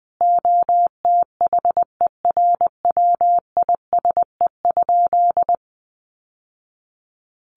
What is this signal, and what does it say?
Morse code, heard as a keyed tone, about 20 wpm, 700 Hz. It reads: OTHERWISE?
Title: Sentences from Top 1200 Words 20wpm